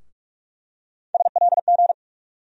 Morse code, heard as a keyed tone, about 42 wpm, 700 Hz. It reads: SLG
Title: USA Call Sign Postfixes - 3 Letters 45wpm